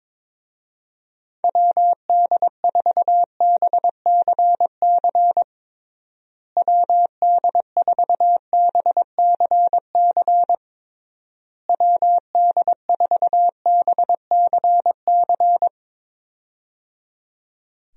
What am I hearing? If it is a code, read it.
Morse code, heard as a keyed tone, about 22 wpm, 700 Hz. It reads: WD4BCC WD4BCC WD4BCC
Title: USA Call Signs - 6 Characters - Repeated 3x - 22wpm